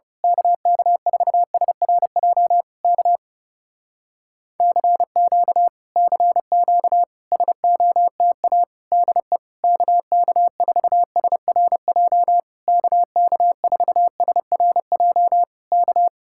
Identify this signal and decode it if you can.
Morse code, heard as a keyed tone, about 28 wpm, 700 Hz. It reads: KK4SRJ K CQ CQ SOTA DE KK4SRJ KK4SRJ K
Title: Speed-Racing Common QSO Phrases - 30wpm